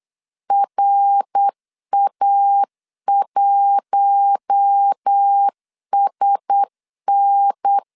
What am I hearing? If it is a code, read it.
Morse code, heard as a keyed tone, about 8 wpm, 800 Hz. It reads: RA1SN
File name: 8wpm